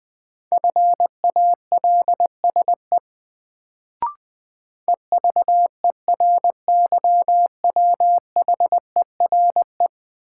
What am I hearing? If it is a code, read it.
Morse code, heard as a keyed tone, about 21 wpm, 700 Hz. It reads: FALSE EVERYWHERE
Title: Sentences from Top 1600 Words - New Word Drill Alt - 20wpm